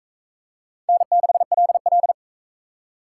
Morse code, heard as a keyed tone, about 40 wpm, 700 Hz. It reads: N6LL